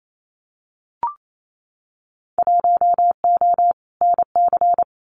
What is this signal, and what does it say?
Morse code, heard as a keyed tone, about 28 wpm, 690 Hz. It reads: 1O NC
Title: Winter Field Day Exchanges Encore 28wpm